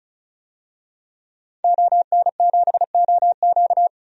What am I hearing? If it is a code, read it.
Morse code, heard as a keyed tone, about 36 wpm, 700 Hz. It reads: ON7OQ
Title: DXCC-100 Call Signs - 5 Characters - 35wpm